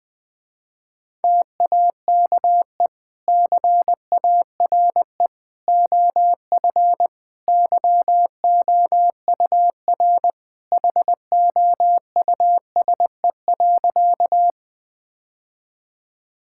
Morse code, heard as a keyed tone, about 20 wpm, 700 Hz. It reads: TAKE CARE OF YOUR HOUSE.
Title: Sentences with 2-Word Phrases - Spoken Beginning - 20wpm